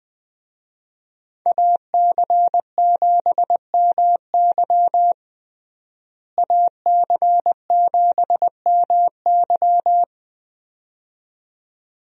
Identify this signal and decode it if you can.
Morse code, heard as a keyed tone, about 20 wpm, 700 Hz. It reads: AC7MY AC7MY
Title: USA Call Signs - 5 Characters - Repeated 2x - 20wpm